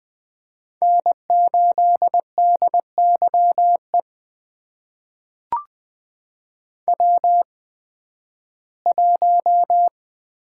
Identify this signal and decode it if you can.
Morse code, heard as a keyed tone, about 20 wpm, spaced as for 13 wpm, 700 Hz. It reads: N8DYE W1
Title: USA Call Signs - 5 Characters - 8x Character Spacing - 20wpm